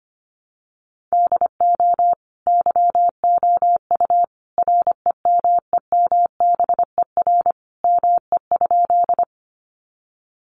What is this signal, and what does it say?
Morse code, heard as a keyed tone, about 25 wpm, 700 Hz. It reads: DO YOU REMEMBER ME?
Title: Sentences from Top 400 Words 25wpm